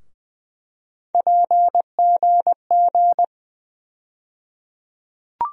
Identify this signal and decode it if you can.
Morse code, heard as a keyed tone, about 20 wpm, 700 Hz. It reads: PGG
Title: USA Call Sign Postfixes - 3 Letters 20wpm